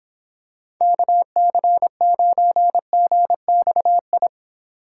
Morse code, heard as a keyed tone, about 26 wpm, 700 Hz. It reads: KC9GXI